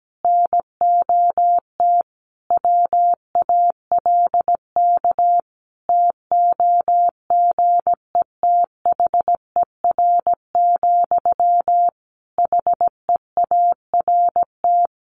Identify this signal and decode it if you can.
Morse code, heard as a keyed tone, about 17 wpm, 700 Hz. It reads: NOT WALK TOGETHER, HEART
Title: Sentences from Top 1500 Words 17wpm